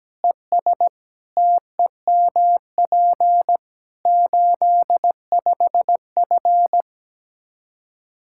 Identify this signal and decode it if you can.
Morse code, heard as a keyed tone, about 17 wpm, 700 Hz. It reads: ES TEMP 85F
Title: Practice QSO - Exchange 2 - Info - 17wpm